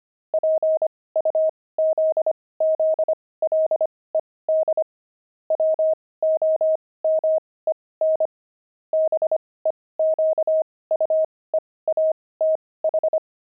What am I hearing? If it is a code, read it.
Morse code, heard as a keyed tone, about 25 wpm, spaced as for 18 wpm, 630 Hz. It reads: PUZZLED WOMEN BEQUEATH
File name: pangram-fs-18-cs-25-jerks